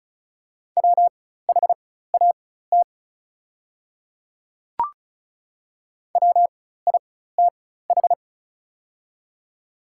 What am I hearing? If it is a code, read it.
Morse code, heard as a keyed tone, about 36 wpm, spaced as for 16 wpm, 700 Hz. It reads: WHAT WITH
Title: 4x Character Spacing - Top 100 Words - 35wpm